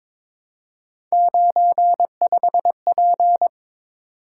Morse code, heard as a keyed tone, about 22 wpm, 700 Hz. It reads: 95P
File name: practice / 22wpm / Three Letter-Number Combinations - 22wpm